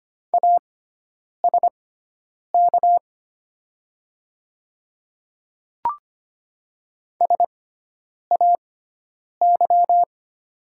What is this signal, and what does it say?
Morse code, heard as a keyed tone, about 25 wpm, spaced as for 9 wpm, 700 Hz. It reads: ASK SAY
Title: 6x Character Spacing - Top 200 Words - 25wpm